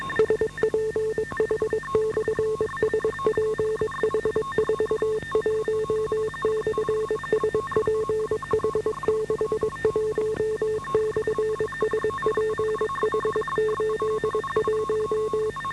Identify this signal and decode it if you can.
Morse code, heard as a keyed tone, about 22 wpm, 430 Hz, with noise. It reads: SPH/SPH41/SPH61/SPH81